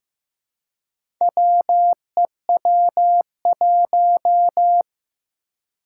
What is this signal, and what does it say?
Morse code, heard as a keyed tone, about 15 wpm, 700 Hz. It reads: WEW1
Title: Four Challenging Letter-Number Combinations 15wpm